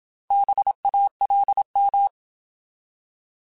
Morse code, heard as a keyed tone, about 27 wpm, 800 Hz. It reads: BALM
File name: Alphabetical / Alphabetical mp3s / Bs @25